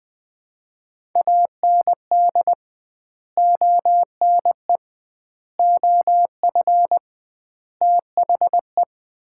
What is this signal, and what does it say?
Morse code, heard as a keyed tone, about 20 wpm, 700 Hz. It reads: AND ONE OF THE
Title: 2x Word-Spacing - Sets of 5 Words - 20wpm